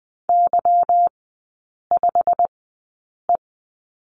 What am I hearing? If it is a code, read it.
Morse code, heard as a keyed tone, about 20 wpm, spaced as for 8 wpm, 690 Hz. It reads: Y5E
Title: 145 Lesson 24e - Letter y - Three Characters 20wpm